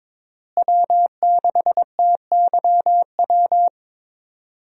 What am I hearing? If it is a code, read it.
Morse code, heard as a keyed tone, about 21 wpm, 700 Hz. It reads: W6TYW
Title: USA Call Signs - 5 Characters - Repeated 2x - 22wpm